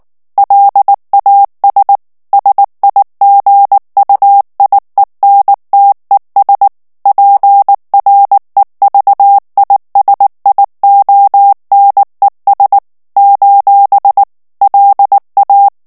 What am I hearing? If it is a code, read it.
Morse code, heard as a keyed tone, about 19 wpm, 800 Hz. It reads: LAS SIGUIENTES PREVISIONES : LA
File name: morse mp3 / ESPAGNOL TEXTE EN CLAIR 2 VITESSE 18 MOTS MINUTES